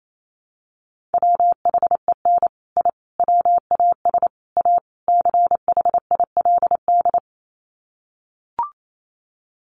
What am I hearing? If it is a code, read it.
Morse code, heard as a keyed tone, about 28 wpm, 690 Hz. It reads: WHEN I WAS A CHILD
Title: Sets of 5 Words Encore 28wpm